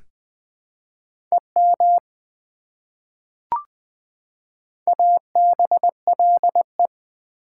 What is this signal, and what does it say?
Morse code, heard as a keyed tone, about 20 wpm, 700 Hz. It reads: EM ABLE
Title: Prefixes and Suffixes 20wpm